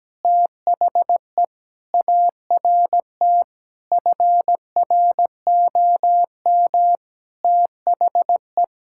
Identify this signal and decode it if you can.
Morse code, heard as a keyed tone, about 17 wpm, 700 Hz. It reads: THE ART FROM THE
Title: Sentences from Top 1000 Words - Review Part 6 - 17wpm